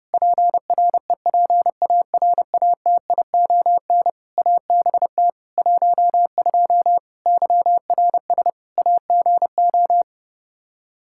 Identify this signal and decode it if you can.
Morse code, heard as a keyed tone, about 30 wpm, 700 Hz. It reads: PREPARATION ABT 12 YRS AGO
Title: K1USN SST QSO Practice 30wpm